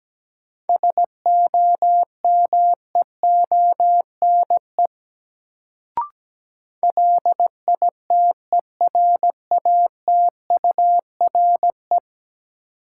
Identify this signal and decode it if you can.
Morse code, heard as a keyed tone, about 17 wpm, 700 Hz. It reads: SOMEONE LITERATURE
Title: Sentences from Top 600 Words - New Word Drill - 17wpm